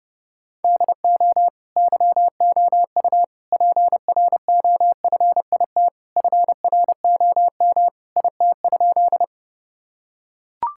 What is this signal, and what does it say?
Morse code, heard as a keyed tone, about 32 wpm, 700 Hz. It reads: DO YOU PROFIT FROM IT?
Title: Sentences from Top 1000 Words - Review Part 5 - 30wpm